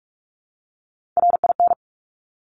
Morse code, heard as a keyed tone, about 46 wpm, 700 Hz. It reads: RIN
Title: Top Three Letter Combinations in Words 45wpm